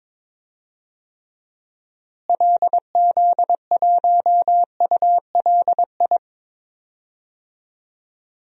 Morse code, heard as a keyed tone, about 21 wpm, 700 Hz. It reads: LZ1ULI
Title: DXCC-100 Call Signs - 6 Characters - Repeated 2x - 22wpm